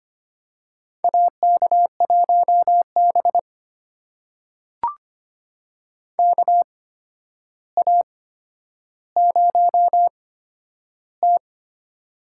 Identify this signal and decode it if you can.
Morse code, heard as a keyed tone, about 25 wpm, spaced as for 11 wpm, 700 Hz. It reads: AK1B KA0T
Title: USA Call Signs - 4 Characters - 8x Character Spacing - 25wpm